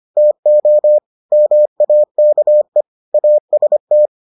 Morse code, heard as a keyed tone, about 25 wpm, 600 Hz. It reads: TO MAKE AST